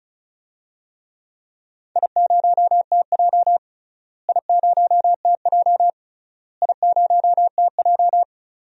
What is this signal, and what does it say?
Morse code, heard as a keyed tone, about 34 wpm, 700 Hz. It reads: I0TJ I0TJ I0TJ
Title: DXCC-100 Call Signs - 4 Characters - Repeated 3x - 35wpm